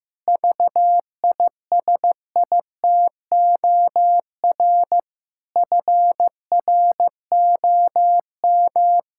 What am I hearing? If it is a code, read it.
Morse code, heard as a keyed tone, about 15 wpm, 700 Hz. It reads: VISITOR FROM
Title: Pangram Challenge v2 - 15wpm